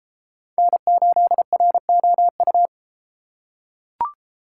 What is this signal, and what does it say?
Morse code, heard as a keyed tone, about 32 wpm, 700 Hz. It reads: N8ROU